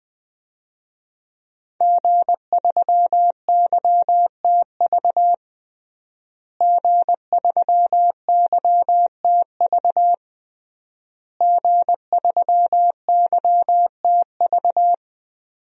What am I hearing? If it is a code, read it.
Morse code, heard as a keyed tone, about 20 wpm, 700 Hz. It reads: G3YTV G3YTV G3YTV
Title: DXCC-100 Call Signs - 5 Characters - Repeated 3x - 20wpm